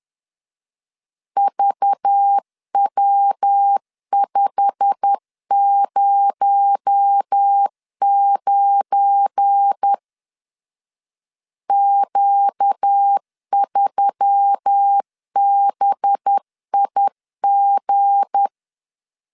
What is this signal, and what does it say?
Morse code, heard as a keyed tone, about 10 wpm, 800 Hz. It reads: VW509 Q3BIG
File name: LEZIONI / 8 YJ45 / 8 YJ45 10wpm 2549